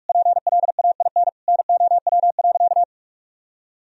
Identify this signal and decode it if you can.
Morse code, heard as a keyed tone, about 47 wpm, 700 Hz. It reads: PLAIN NOW.
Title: Sentences from Top 1000 Words - Review Part 3 - 45wpm